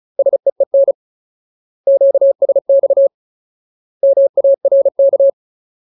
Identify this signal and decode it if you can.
Morse code, heard as a keyed tone, about 34 wpm, 550 Hz. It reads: SEEN QSX MARK
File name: Ultimate Word List / 35WPM Ultimate